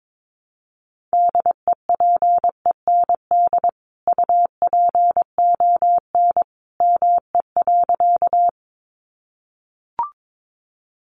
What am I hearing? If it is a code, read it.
Morse code, heard as a keyed tone, about 22 wpm, 700 Hz. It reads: DEPEND UPON ME.
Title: Sentences from Top 1300 Words 22wpm